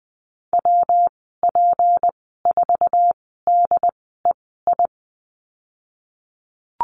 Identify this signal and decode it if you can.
Morse code, heard as a keyed tone, about 20 wpm, spaced as for 15 wpm, 700 Hz. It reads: WP4DEI
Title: 167 Lesson 27g - Number 7 - Call Signs 20wpm